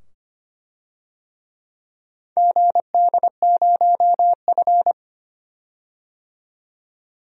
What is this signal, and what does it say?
Morse code, heard as a keyed tone, about 25 wpm, 700 Hz. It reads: GD0F
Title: DXCC-100 Call Signs - 4 Characters - 25wpm